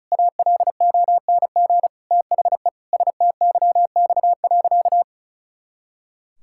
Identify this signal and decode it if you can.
Morse code, heard as a keyed tone, about 34 wpm, 700 Hz. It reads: ALONG THE STYX.
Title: Pangram Challenge v2 - 35wpm